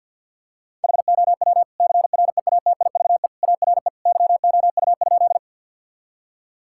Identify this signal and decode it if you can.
Morse code, heard as a keyed tone, about 47 wpm, 700 Hz. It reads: HOW CREATIVE ARE YOU?
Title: Sentences from Top 1000 Words - Review Part 5 - 50wpm